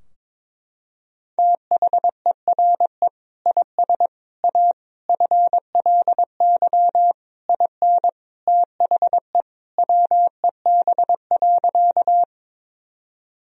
Morse code, heard as a keyed tone, about 22 wpm, 700 Hz. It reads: THERE IS A FLY IN THE WEB.
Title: Sentences from Top 1000 Words - Review Part 4 - 22wpm